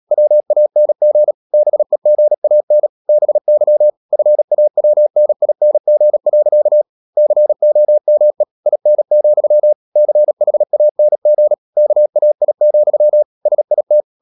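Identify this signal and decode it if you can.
Morse code, heard as a keyed tone, about 37 wpm, 600 Hz. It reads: WANG BEGAN BY FAWNING. COME IN, CHANG KAI, SIT
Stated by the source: Morse code practice at 35–39 wpm sending English text of Ken Follett's novel Never, from CW med Gnister.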